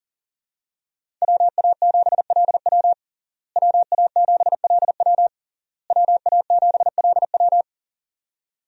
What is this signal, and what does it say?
Morse code, heard as a keyed tone, about 37 wpm, 700 Hz. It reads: WA7LW WA7LW WA7LW
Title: USA Call Signs - 5 Characters - Repeated 3x - 40wpm